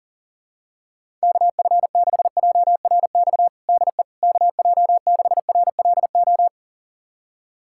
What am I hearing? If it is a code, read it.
Morse code, heard as a keyed tone, about 40 wpm, 700 Hz. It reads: KF6JRX DE KJ6RLO
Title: USA Callsign DE Callsign 40wpm